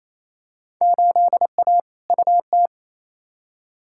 Morse code, heard as a keyed tone, about 28 wpm, 700 Hz. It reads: 8A UT
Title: Field Day Exchanges Encore - 28wpm